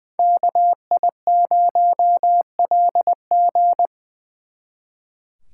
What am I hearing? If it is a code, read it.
Morse code, heard as a keyed tone, about 24 wpm, 700 Hz. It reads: KI0LG
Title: Speed Racing - USA 2x2 Call Signs 20wpm